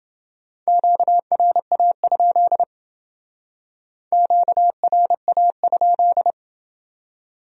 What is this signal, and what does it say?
Morse code, heard as a keyed tone, about 28 wpm, 700 Hz. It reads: QRA? QRA?